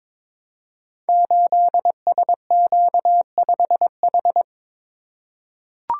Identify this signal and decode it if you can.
Morse code, heard as a keyed tone, about 22 wpm, 700 Hz. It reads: 8SQ5H
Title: Code Groups - 5 Letter:Numbers - 22wpm